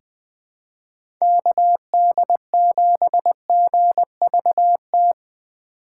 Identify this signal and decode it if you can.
Morse code, heard as a keyed tone, about 20 wpm, 700 Hz. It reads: KD7GVT
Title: USA Call Signs - 6 Characters - 4x Character Spacing - 20wpm